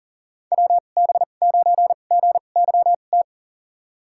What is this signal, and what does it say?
Morse code, heard as a keyed tone, about 40 wpm, spaced as for 29 wpm, 700 Hz. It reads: WB9GYT
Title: USA Call Signs - 6 Characters - 2x Character Spacing - 40wpm